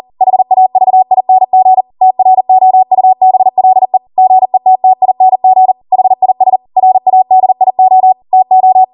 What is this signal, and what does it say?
Morse code, heard as a keyed tone, about 40 wpm, 750 Hz. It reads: HAVING TROUBLE GETTING HIS RADIO TO